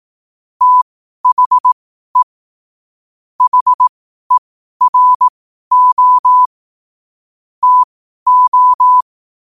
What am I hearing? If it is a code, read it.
Morse code, heard as a keyed tone, about 18 wpm, spaced as for 12 wpm, 1000 Hz. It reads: THE HERO TO
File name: qotdfiles / 12wpm